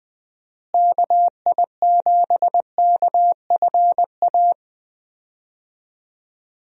Morse code, heard as a keyed tone, about 20 wpm, 700 Hz. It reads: KI7KFA
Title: USA Call Signs - 6 Characters - Repeated 3x - 20wpm